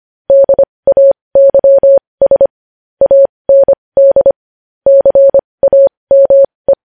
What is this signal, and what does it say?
Morse code, heard as a keyed tone, about 25 wpm, spaced as for 20 wpm, 550 Hz. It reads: DAYS AND CAME
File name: podcast / News headlines-1x-25x20